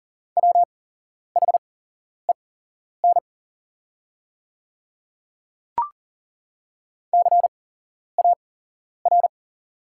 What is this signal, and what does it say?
Morse code, heard as a keyed tone, about 42 wpm, spaced as for 11 wpm, 700 Hz. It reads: WHEN CAR
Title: 8x Character Spacing - Top 300 Words - 40wpm